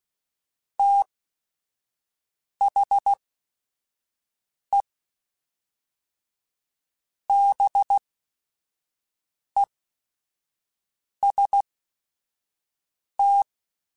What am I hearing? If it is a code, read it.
Morse code, heard as a keyed tone, about 16 wpm, spaced as for 5 wpm, 800 Hz. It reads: THE BEST